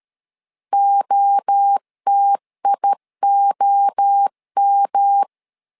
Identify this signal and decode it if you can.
Morse code, heard as a keyed tone, about 13 wpm, 800 Hz. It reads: OTIOM
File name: LEZIONI / 3 MWRO / 3 MWRO 12wpm 1730